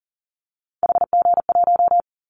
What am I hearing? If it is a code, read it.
Morse code, heard as a keyed tone, about 40 wpm, 700 Hz. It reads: HG1
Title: Callsign Prefixes for Top 25 DXCC - Repeated 3x - 40wpm